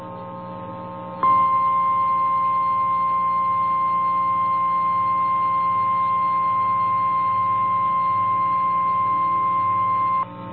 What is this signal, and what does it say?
Emergency Alert System (SAME) tones (the 1050 Hz attention tone). It is heard from 1.2 to 10.3 s.
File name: eas / KNCYKBIE / KNCYKBIE